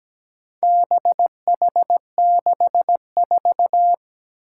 Morse code, heard as a keyed tone, about 17 wpm, 700 Hz. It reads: BH64
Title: Four Challenging Letter-Number Combinations 17wpm